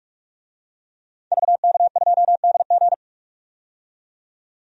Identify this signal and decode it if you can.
Morse code, heard as a keyed tone, about 46 wpm, 700 Hz. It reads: VK2DG
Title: DXCC-100 Call Signs - 5 Characters - 45wpm